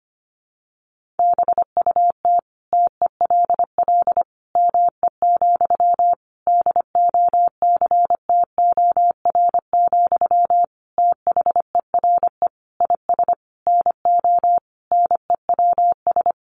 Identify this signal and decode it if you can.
Morse code, heard as a keyed tone, about 25 wpm, 700 Hz. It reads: BUT TELL ME, DOCTOR, THERE IS NO NEWS
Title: Sentences from Top 1500 Words 25wpm